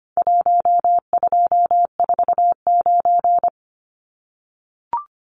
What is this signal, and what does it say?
Morse code, heard as a keyed tone, about 25 wpm, 700 Hz. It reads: 1249
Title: Numbers - 4-Digit 25wpm